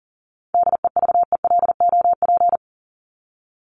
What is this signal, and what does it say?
Morse code, heard as a keyed tone, about 40 wpm, 700 Hz. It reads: DEVELOP
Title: ED Suffix - Top 100 Words with Base Words 40wpm